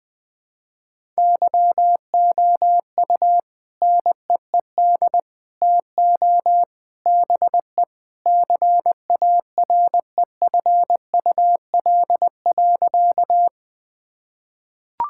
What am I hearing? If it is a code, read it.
Morse code, heard as a keyed tone, about 19 wpm, 700 Hz. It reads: YOU NEED TO BE CAREFUL.
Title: Sentences with 3-Word Phrases - 20wpm